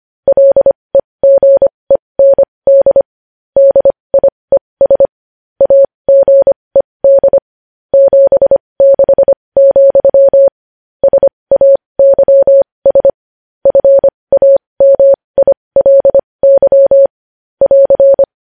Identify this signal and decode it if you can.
Morse code, heard as a keyed tone, about 25 wpm, spaced as for 20 wpm, 550 Hz. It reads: LEGEND DIES AGED 76, SAYS FAMILY <AR>
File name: podcast / News headlines-1x-25x20